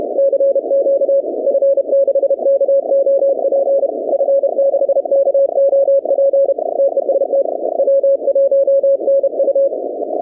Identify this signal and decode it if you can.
Morse code, heard as a keyed tone, about 32 wpm, 550 Hz. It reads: CQ F6KOP F6KOP TEST W1NU
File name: contests / 2019-cw-wpx-kt5j-k5pi- -2330 0-15